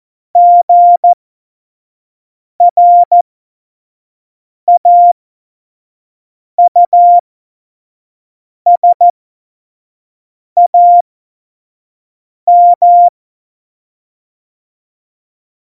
Morse code, heard as a keyed tone, about 14 wpm, spaced as for 5 wpm, 700 Hz. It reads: GRAUSAM
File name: Morsekurs / Witze / Witz 24 April 2018, 25 F70